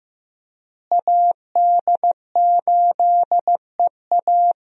Morse code, heard as a keyed tone, about 15 wpm, 700 Hz. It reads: AD8EA
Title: USA Call Signs - 5 Characters - Repeated 3x - 15wpm